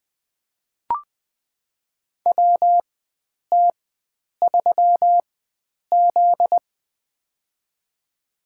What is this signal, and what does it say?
Morse code, heard as a keyed tone, about 20 wpm, spaced as for 9 wpm, 700 Hz. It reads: WT3Z
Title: USA Call Signs - 4 Characters - 4x Character Spacing - 20wpm